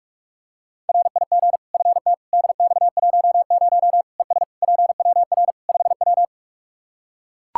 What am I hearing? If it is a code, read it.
Morse code, heard as a keyed tone, about 41 wpm, 700 Hz. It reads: RIG FT DX10 ES PWR 5W
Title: Practice QSO - Exchange 2 - Info - 45wpm